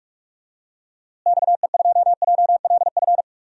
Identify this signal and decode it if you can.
Morse code, heard as a keyed tone, about 47 wpm, 700 Hz. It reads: XE2JLF
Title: DXCC-100 Call Signs - 6 Characters - Repeated 3x - 45wpm